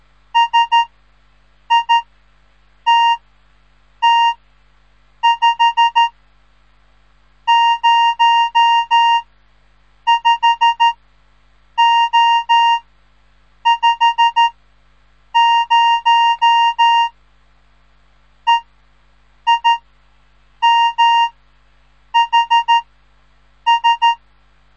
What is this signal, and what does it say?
Morse code, heard as a keyed tone, about 13 wpm, 930 Hz. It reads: S I T T 5 0 5 O 5 0 E I M H S
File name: radiomorningstar / morse / les 3